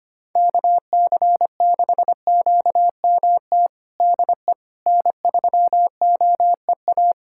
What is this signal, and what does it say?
Morse code, heard as a keyed tone, about 25 wpm, 700 Hz. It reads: KC6QMT DE N3OEA
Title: USA Callsign DE Callsign 25wpm